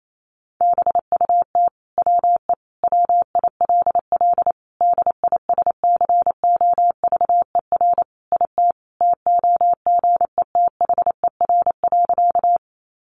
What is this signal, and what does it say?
Morse code, heard as a keyed tone, about 28 wpm, 700 Hz. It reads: BUT WE WILL DISCOVER IT TOGETHER.
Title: Sentences from Top 1100 Words 28wpm